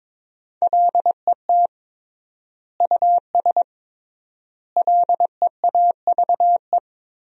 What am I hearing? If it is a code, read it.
Morse code, heard as a keyed tone, about 22 wpm, 700 Hz. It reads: LET US LEAVE
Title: 3x Word-Spacing - Sentences from Top 400 Words - 22wpm